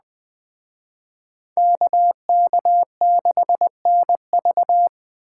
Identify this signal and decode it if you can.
Morse code, heard as a keyed tone, about 20 wpm, 700 Hz. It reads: KK6NV
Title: Speed Racing - USA 2x2 Call Signs 17wpm